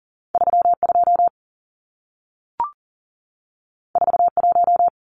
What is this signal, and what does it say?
Morse code, heard as a keyed tone, about 40 wpm, 700 Hz. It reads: 32 41
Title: Numbers - 2-Digit 40wpm